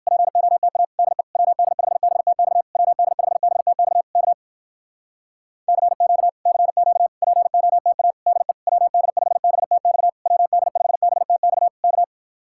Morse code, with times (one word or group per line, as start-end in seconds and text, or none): 0.1-0.8 POTA
1.0-1.2 DE
1.4-2.6 WD5BTX
2.8-4.0 WD5BTX
4.2-4.3 K
5.7-6.3 CQ
6.5-7.1 CQ
7.2-8.1 POTA
8.3-8.5 DE
8.7-10.1 WD5BTX
10.3-11.7 WD5BTX
11.8-12.0 K